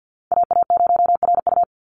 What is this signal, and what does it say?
Morse code, heard as a keyed tone, about 50 wpm, 700 Hz. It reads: AA0RU
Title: USA Call Signs 50wpm